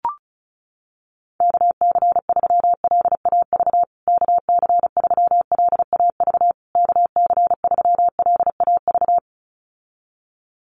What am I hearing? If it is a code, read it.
Morse code, heard as a keyed tone, about 34 wpm, 700 Hz. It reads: KC3LAV KC3LAV KC3LAV
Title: USA Call Signs Repeated 3x 35wpm